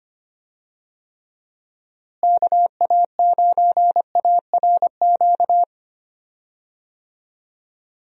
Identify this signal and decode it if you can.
Morse code, heard as a keyed tone, about 25 wpm, 700 Hz. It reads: KA9ARQ
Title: USA Call Signs - 6 Characters - Repeated 2x - 25wpm